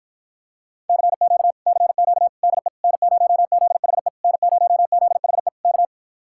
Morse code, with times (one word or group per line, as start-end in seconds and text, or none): 0.9-1.5 CQ
1.7-2.3 CQ
2.4-2.7 DE
2.8-4.1 N0ZHE
4.2-5.5 N0ZHE
5.6-5.8 K